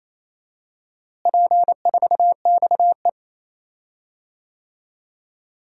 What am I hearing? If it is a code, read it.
Morse code, heard as a keyed tone, about 28 wpm, 700 Hz. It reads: P4XE
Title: DXCC-100 Call Signs - 4 Characters - Repeated 3x - 28wpm